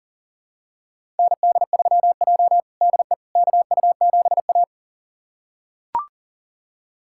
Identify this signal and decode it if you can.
Morse code, heard as a keyed tone, about 40 wpm, 700 Hz. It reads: ND3J DE KU7A
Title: USA Callsign DE Callsign - 4 Characters - 40wpm